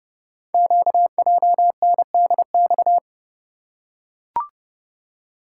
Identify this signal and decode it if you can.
Morse code, heard as a keyed tone, about 30 wpm, 690 Hz. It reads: QJNDX
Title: Code Groups - 5 Letters - 30wpm